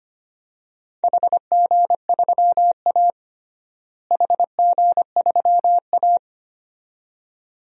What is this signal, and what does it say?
Morse code, heard as a keyed tone, about 25 wpm, 700 Hz. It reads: HG3A HG3A
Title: DXCC-100 Call Signs - 4 Characters - Repeated 2x - 25wpm